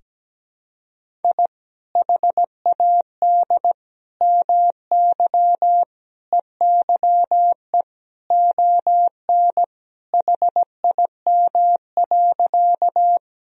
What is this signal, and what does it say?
Morse code, heard as a keyed tone, about 17 wpm, 700 Hz. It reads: I HAD MY EYE ON HIM.
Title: Sentences from Top 1000 Words - Review Part 1 - 17wpm